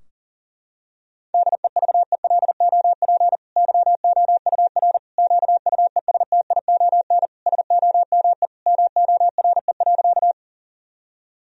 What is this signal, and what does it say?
Morse code, heard as a keyed tone, about 40 wpm, 700 Hz. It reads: DEVELOP YOUR QUESTION SOME MORE.
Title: Sentences from Top 1000 Words - Review Part 2 - 40wpm